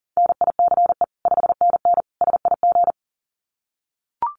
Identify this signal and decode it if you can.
Morse code, heard as a keyed tone, about 40 wpm, 700 Hz. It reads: NICE 5NN SIG
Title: Common QSO Phrases 40wpm